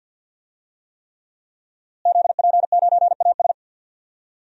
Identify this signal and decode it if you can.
Morse code, heard as a keyed tone, about 50 wpm, 700 Hz. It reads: ZP9AS
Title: DXCC-100 Call Signs - 5 Characters - Repeated 3x - 50wpm